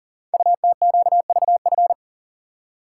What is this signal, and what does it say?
Morse code, heard as a keyed tone, about 38 wpm, 690 Hz. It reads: UTQVF
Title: Code Groups - 5 Letters - 40wpm